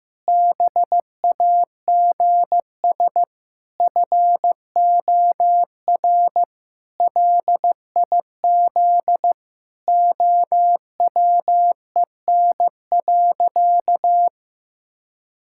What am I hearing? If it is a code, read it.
Morse code, heard as a keyed tone, about 15 wpm, 700 Hz. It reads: BAGS FOR LIZ OWEN.
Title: Pangram Challenge v2 - 15wpm